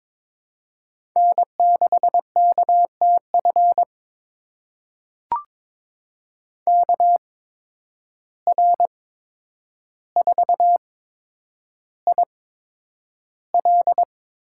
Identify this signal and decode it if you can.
Morse code, heard as a keyed tone, about 22 wpm, spaced as for 9 wpm, 700 Hz. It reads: N6KTF KR4IL
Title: USA Call Signs - 8x Character Spacing - 22wpm